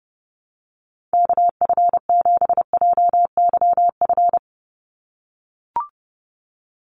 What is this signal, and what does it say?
Morse code, heard as a keyed tone, about 30 wpm, 700 Hz. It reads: KF7JYF
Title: USA Call Signs 30wpm